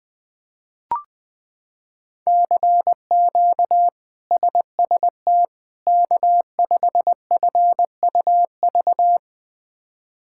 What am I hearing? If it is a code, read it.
Morse code, heard as a keyed tone, about 20 wpm, 700 Hz. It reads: CQ SST K5FUV
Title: K1USN SST QSO Practice - 20wpm